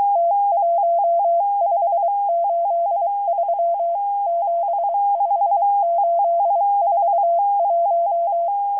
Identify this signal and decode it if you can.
Morse code, heard as a keyed tone, about 23 wpm, 680 Hz. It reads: T1583 75841